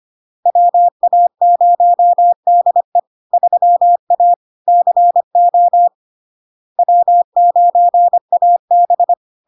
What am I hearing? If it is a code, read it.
Morse code, heard as a keyed tone, about 25 wpm, 700 Hz. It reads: WA0DE 3A CO W9AB
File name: LARK-FD-2022-Log-in-CW-25wpm-700-Hz